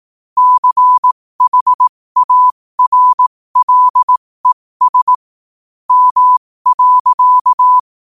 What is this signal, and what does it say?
Morse code, heard as a keyed tone, about 18 wpm, spaced as for 15 wpm, 1000 Hz. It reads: CHARLES M.
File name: qotdfiles / 15wpm